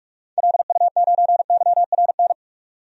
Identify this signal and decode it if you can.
Morse code, heard as a keyed tone, about 42 wpm, 700 Hz. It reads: LU9YRN
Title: DXCC-100 Call Signs - 6 Characters - 45wpm